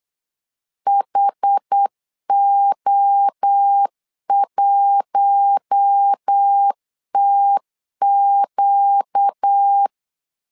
Morse code, heard as a keyed tone, about 8 wpm, 800 Hz. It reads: HO1TQ
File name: LEZIONI / 4 Q1 8wpm 3035